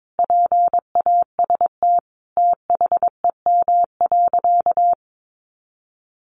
Morse code, heard as a keyed tone, about 22 wpm, 700 Hz. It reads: PAST THEM.
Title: Sentences from Top 600 Words 22wpm